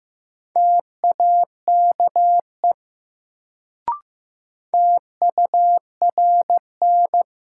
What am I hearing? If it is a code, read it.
Morse code, heard as a keyed tone, about 15 wpm, 700 Hz. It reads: TAKE TURN
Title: Sentences from Top 200 Words - New Word Drill - 15wpm